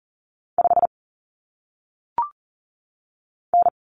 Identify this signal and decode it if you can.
Morse code, heard as a keyed tone, about 40 wpm, 700 Hz. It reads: 5 N
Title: Common Cut Numbers - 1-Digit 40wpm